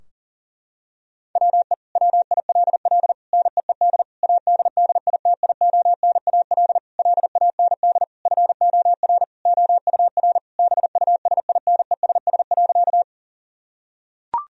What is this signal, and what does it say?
Morse code, heard as a keyed tone, about 37 wpm, 700 Hz. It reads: WE WILL NEED ADDITIONAL LAND FOR OUR BUSINESS.
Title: Sentences from Top 1000 Words - Review Part 4 - 40wpm